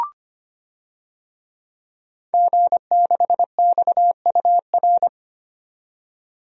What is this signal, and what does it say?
Morse code, heard as a keyed tone, about 25 wpm, 700 Hz. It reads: G6XUR
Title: DXCC-100 Call Signs - 5 Characters - 25wpm